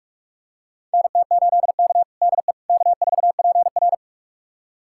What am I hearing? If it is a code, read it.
Morse code, heard as a keyed tone, about 42 wpm, 700 Hz. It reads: NT8K DE K4PR
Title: USA Callsign DE Callsign - 4 Characters - 45wpm